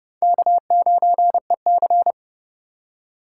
Morse code, heard as a keyed tone, about 29 wpm, 700 Hz. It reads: K9EC
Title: USA Call Signs - 4 Characters - 2x Character Spacing - 30wpm